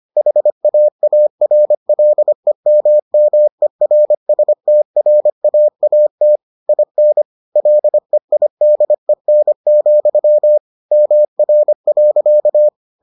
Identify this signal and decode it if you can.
Morse code, heard as a keyed tone, about 25 wpm, 600 Hz. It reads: HAARLEMMERSTRAAT IN LEIDEN, MR.